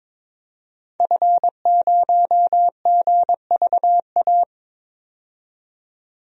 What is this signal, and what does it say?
Morse code, heard as a keyed tone, about 22 wpm, 700 Hz. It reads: F0GVA
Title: DXCC-100 Call Signs - 5 Characters - 22wpm